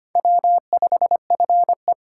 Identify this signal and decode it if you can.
Morse code, heard as a keyed tone, about 32 wpm, 700 Hz. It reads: W5FE